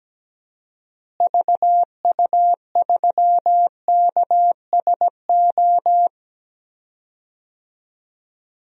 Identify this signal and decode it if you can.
Morse code, heard as a keyed tone, about 17 wpm, 700 Hz. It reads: VU3KSO
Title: DXCC-100 Call Signs - 6 Characters - Repeated 2x - 17wpm